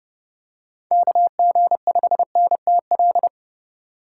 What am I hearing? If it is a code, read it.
Morse code, heard as a keyed tone, about 32 wpm, 700 Hz. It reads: KG5NTL
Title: USA Call Signs - 6 Characters - 4x Character Spacing - 30wpm